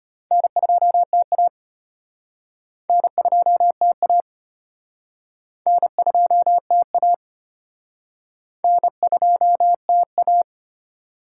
Morse code, heard as a keyed tone, about 36 wpm, 700 Hz. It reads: N2TA N2TA N2TA N2TA